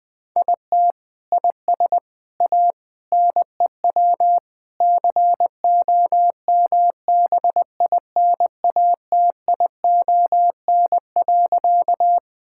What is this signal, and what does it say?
Morse code, heard as a keyed tone, about 21 wpm, 700 Hz. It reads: IT IS A NEW COMBINATION.
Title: Sentences from Top 1000 Words - Review Part 6 - 20wpm